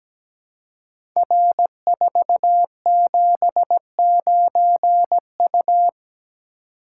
Morse code, heard as a keyed tone, about 17 wpm, 700 Hz. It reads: R479U
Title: Code Groups - 5 Letter:Numbers - 17wpm